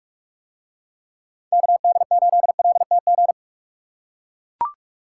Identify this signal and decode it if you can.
Morse code, heard as a keyed tone, about 44 wpm, 700 Hz. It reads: KD8LTG